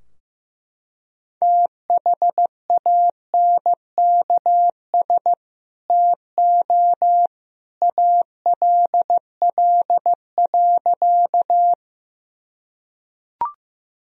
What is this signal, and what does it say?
Morse code, heard as a keyed tone, about 15 wpm, 700 Hz. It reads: THANKS TO ALL.
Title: Sentences from Top 1000 Words - Review Part 3 - 15wpm